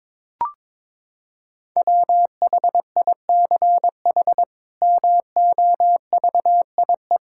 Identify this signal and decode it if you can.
Morse code, heard as a keyed tone, about 22 wpm, 700 Hz. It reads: WHICH MOVIE
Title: Sentences from Top 1000 Words - Review Part 5 - 22wpm